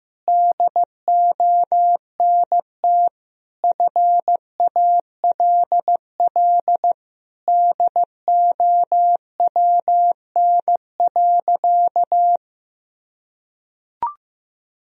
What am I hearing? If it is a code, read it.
Morse code, heard as a keyed tone, about 16 wpm, 700 Hz. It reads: DONT FALL DOWN.
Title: Sentences from Top 1000 Words - Review Part 2 - 15wpm